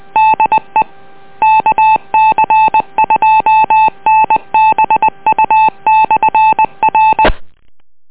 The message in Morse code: DE KC2NBU/R